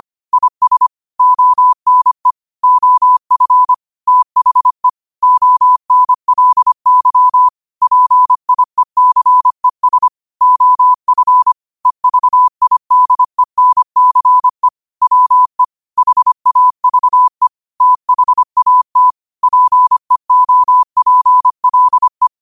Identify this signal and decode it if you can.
Morse code, heard as a keyed tone, about 25 wpm, 1000 Hz. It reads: IS ONE OF THE ONLY PIECES OF EVIDENCE WE HAVE THAT PEOPLE